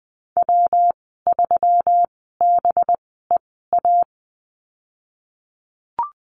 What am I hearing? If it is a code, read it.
Morse code, heard as a keyed tone, about 20 wpm, spaced as for 15 wpm, 700 Hz. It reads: W3BEA
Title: 213 Lesson 34g - Number 8 - Call Signs 20wpm